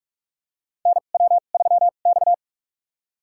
Morse code, heard as a keyed tone, about 44 wpm, spaced as for 32 wpm, 700 Hz. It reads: NW3X
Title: USA Call Signs - 4 Characters - 2x Character Spacing - 45wpm